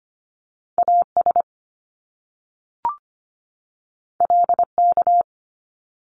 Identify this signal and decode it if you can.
Morse code, heard as a keyed tone, about 25 wpm, 700 Hz. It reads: AS LK
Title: Top Two Letter Combinations in Words 25wpm